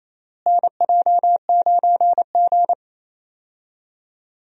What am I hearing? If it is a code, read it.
Morse code, heard as a keyed tone, about 28 wpm, 700 Hz. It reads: NJ9G
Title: USA Call Signs - 4 Characters - Repeated 2x - 28wpm